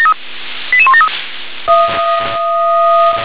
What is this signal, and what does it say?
Five-tone selective call (ZVEI), heard as quick runs of five tones, tones of about 70 ms. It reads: 88173
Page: SDR/ZVEI